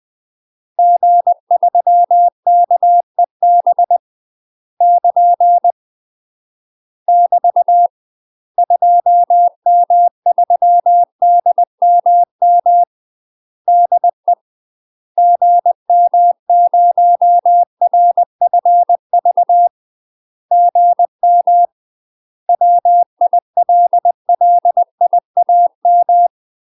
Morse code, heard as a keyed tone, about 20 wpm, 700 Hz. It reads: G3KEB <KN> <BT> 2M3DMM DE GM0RFV GM WILLIAM
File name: UK MASTER QSOs TRAINING/20WPM